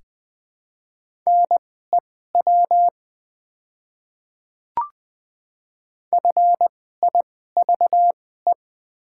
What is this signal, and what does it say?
Morse code, heard as a keyed tone, about 20 wpm, spaced as for 15 wpm, 700 Hz. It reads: NEW FIVE
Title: 2x Character Spacing - Top 500 Words - 20wpm